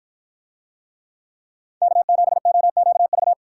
Morse code, heard as a keyed tone, about 48 wpm, 700 Hz. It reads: K7OQV